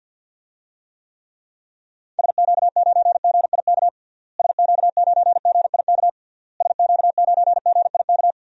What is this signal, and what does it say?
Morse code, heard as a keyed tone, about 47 wpm, 700 Hz. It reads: SQ9GIK SQ9GIK SQ9GIK
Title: DXCC-100 Call Signs - 6 Characters - Repeated 3x - 50wpm